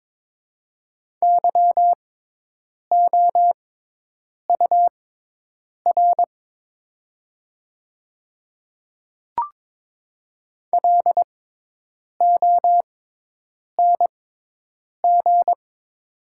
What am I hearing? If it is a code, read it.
Morse code, heard as a keyed tone, about 22 wpm, spaced as for 8 wpm, 690 Hz. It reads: YOUR LONG
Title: 6x Character Spacing - Top 100 Words - 22wpm